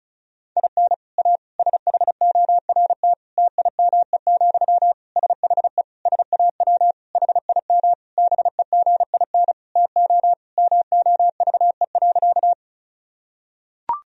Morse code, heard as a keyed tone, about 36 wpm, 700 Hz. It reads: IN A SHORT TIME, SHE SAW HIM BEGIN TO MOVE.
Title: Sentences from Top 1000 Words - Review Part 2 - 35wpm